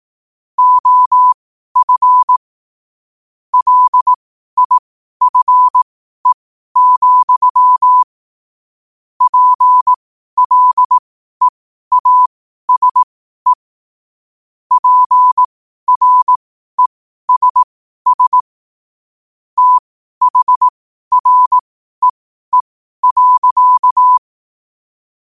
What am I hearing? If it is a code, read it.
Morse code, heard as a keyed tone, about 18 wpm, spaced as for 12 wpm, 1000 Hz. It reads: OF LIFE, PLEASE PRESS THREE.
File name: qotdfiles / 12wpm